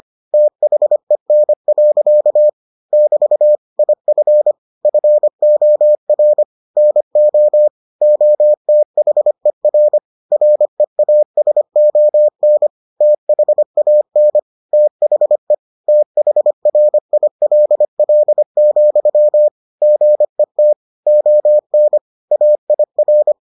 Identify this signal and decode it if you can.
Morse code, heard as a keyed tone, about 25 wpm, 600 Hz. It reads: THEN. <BT> IF FOR NO OTHER REASON THAN THE THRILL, GET ON AIR